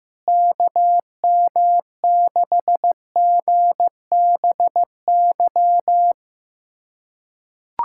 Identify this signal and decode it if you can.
Morse code, heard as a keyed tone, about 15 wpm, 700 Hz. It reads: KM6GBY
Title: USA Callsign DE Callsign - 6 Characters - 15wpm